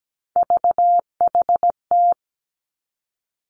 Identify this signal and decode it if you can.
Morse code, heard as a keyed tone, about 17 wpm, 700 Hz. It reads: VHT
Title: Three Challenging Letter-Number Combinations 17wpm